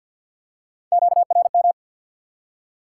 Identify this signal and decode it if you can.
Morse code, heard as a keyed tone, about 50 wpm, 700 Hz. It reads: QRM
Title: Top 90 QSO Elements 50wpm